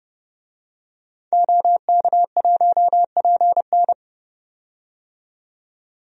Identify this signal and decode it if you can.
Morse code, heard as a keyed tone, about 32 wpm, 700 Hz. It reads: OK1PN
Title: DXCC-100 Call Signs - 5 Characters - 30wpm